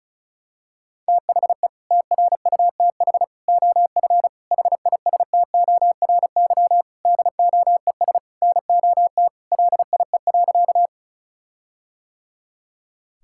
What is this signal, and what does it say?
Morse code, heard as a keyed tone, about 36 wpm, 700 Hz. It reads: THE TRUTH OF HISTORY DOES NOT LIE.
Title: Sentences from Top 1600 Words 35wpm